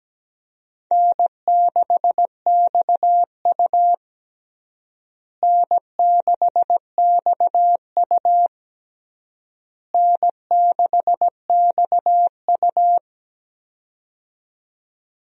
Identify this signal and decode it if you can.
Morse code, heard as a keyed tone, about 17 wpm, 700 Hz. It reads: N6XU N6XU N6XU
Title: USA Call Signs - 4 Characters - Repeated 3x - 17wpm